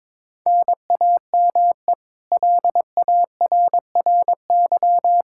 Morse code, heard as a keyed tone, about 22 wpm, 700 Hz. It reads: NAME LARRY
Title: Practice QSO - Exchange 1 - Beginning - 22wpm